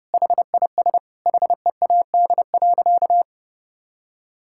Morse code, heard as a keyed tone, about 32 wpm, 700 Hz. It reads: HIS HEAD.
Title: Sentences from Top 1000 Words - Review Part 6 - 30wpm